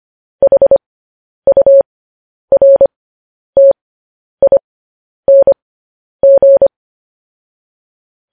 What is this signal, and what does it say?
Morse code, heard as a keyed tone, about 25 wpm, spaced as for 10 wpm, 550 Hz. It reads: HURTING
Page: News Headlines in Morse Code at 10 WPM